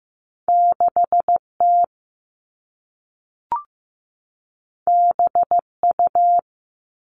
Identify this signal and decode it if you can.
Morse code, heard as a keyed tone, about 15 wpm, 700 Hz. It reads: 6T BU
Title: All Cut Numbers - 2-Digit 15wpm